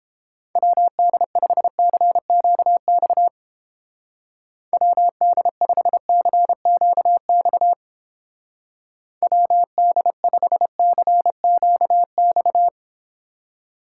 Morse code, heard as a keyed tone, about 31 wpm, 700 Hz. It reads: WD5CQX WD5CQX WD5CQX